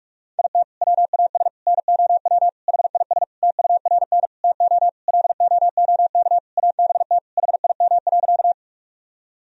Morse code, heard as a keyed tone, about 46 wpm, 700 Hz. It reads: IT WAS NOW HIS TURN TO LOOK ABT HIM.
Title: Sentences from Top 200 Words and Abbreviations 45wpm